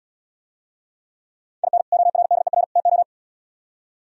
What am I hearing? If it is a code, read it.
Morse code, heard as a keyed tone, about 77 wpm, 700 Hz. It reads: IN ORDER TO